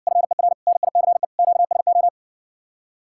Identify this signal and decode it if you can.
Morse code, heard as a keyed tone, about 57 wpm, 700 Hz. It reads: FER NICE QSO